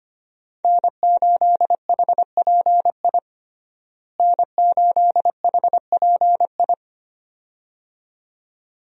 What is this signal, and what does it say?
Morse code, heard as a keyed tone, about 25 wpm, 700 Hz. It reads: N8HPI N8HPI
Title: USA Call Signs - 5 Characters - Repeated 2x - 25wpm